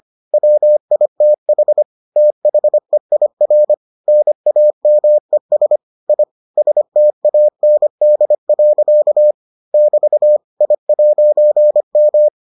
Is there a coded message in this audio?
Yes: Morse code from 0.3 to 12.3 s, heard as a keyed tone, about 25 wpm, 600 Hz. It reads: WITH THEIR NAMES I STAND. <BT> I'M